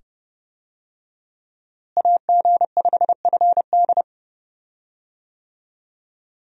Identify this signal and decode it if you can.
Morse code, heard as a keyed tone, about 31 wpm, 700 Hz. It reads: AG5FD
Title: USA Call Signs - 5 Characters - Repeated 2x - 30wpm